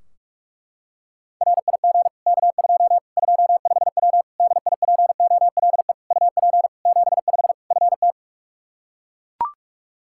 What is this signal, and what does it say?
Morse code, heard as a keyed tone, about 41 wpm, 700 Hz. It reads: RIG K2 25W DIPOLE UP 75 FT
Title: Practice QSO - Exchange 2 - Info - 45wpm